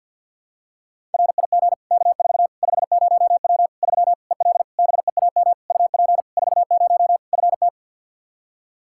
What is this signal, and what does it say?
Morse code, heard as a keyed tone, about 53 wpm, 700 Hz. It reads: RIG K4 50W 3 EL BEAM UP 40 FT
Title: Practice QSO - Exchange 2 - Info - 50wpm